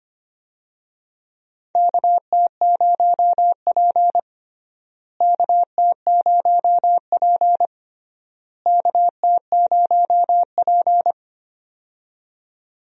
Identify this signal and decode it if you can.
Morse code, heard as a keyed tone, about 25 wpm, 700 Hz. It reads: KT0P KT0P KT0P
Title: USA Call Signs - 4 Characters - Repeated 3x - 25wpm